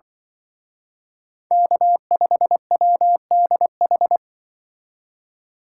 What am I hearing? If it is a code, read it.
Morse code, heard as a keyed tone, about 25 wpm, 700 Hz. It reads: K5WDH
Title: Speed Racing - USA 1x3 Call Signs 20wpm